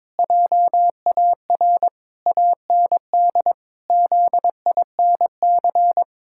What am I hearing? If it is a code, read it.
Morse code, heard as a keyed tone, about 22 wpm, 700 Hz. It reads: JAR AND ZINC
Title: Pangram Challenge v2 - 22wpm